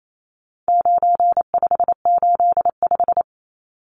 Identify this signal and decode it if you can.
Morse code, heard as a keyed tone, about 28 wpm, 700 Hz. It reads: 9585
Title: Numbers - 4-Digit 28wpm